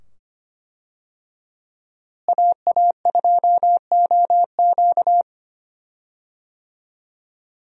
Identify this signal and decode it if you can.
Morse code, heard as a keyed tone, about 25 wpm, 700 Hz. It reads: AA2OQ
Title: USA Call Signs - 5 Characters - Repeated 2x - 25wpm